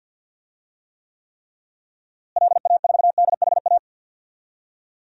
Morse code, heard as a keyed tone, about 49 wpm, 700 Hz. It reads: LA4DHA
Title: DXCC-100 Call Signs - 6 Characters - Repeated 3x - 50wpm